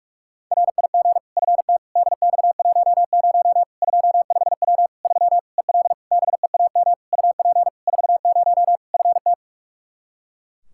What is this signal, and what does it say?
Morse code, heard as a keyed tone, about 42 wpm, 700 Hz. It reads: RIG FT DX10 25W 3 EL BEAM UP 40 FT
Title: Practice QSO - Exchange 2 - Info - 45wpm